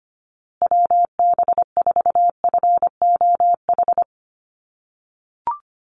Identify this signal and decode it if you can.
Morse code, heard as a keyed tone, about 25 wpm, 700 Hz. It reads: WB4FOH
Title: USA Call Signs Repeated 3x 25wpm